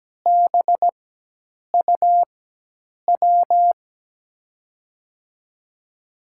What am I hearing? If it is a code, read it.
Morse code, heard as a keyed tone, about 17 wpm, spaced as for 8 wpm, 700 Hz. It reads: BUW
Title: USA Call Signs - 6 Characters - 4x Character Spacing - 17wpm